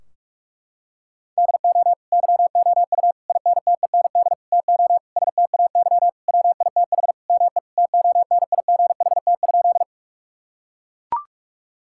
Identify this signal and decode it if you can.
Morse code, heard as a keyed tone, about 41 wpm, 700 Hz. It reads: DO YOU INTEND TO STAY WITH ME TONIGHT?
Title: Sentences from Top 1600 Words 45wpm